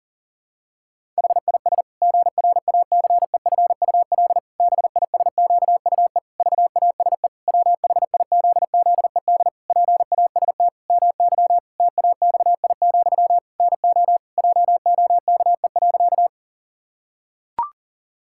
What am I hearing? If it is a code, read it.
Morse code, heard as a keyed tone, about 40 wpm, 700 Hz. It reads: HIS GRACEFUL BISQUE VASE WHIZZED PAST MY TAXI, NO JOKE.
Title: Pangram Challenge v2 - 40wpm